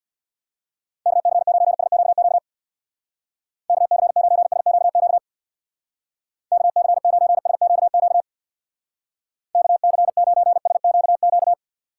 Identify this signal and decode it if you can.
Morse code, heard as a keyed tone, about 63 wpm, 700 Hz. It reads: KC9SQQ KC9SQQ KC9SQQ KC9SQQ